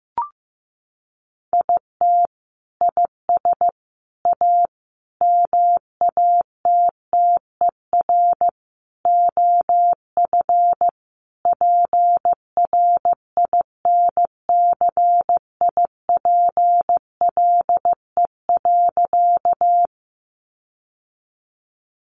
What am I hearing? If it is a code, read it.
Morse code, heard as a keyed tone, about 15 wpm, 700 Hz. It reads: IT IS A MATTER OF PRINCIPLE.
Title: Sentences from Top 1400 Words 15wpm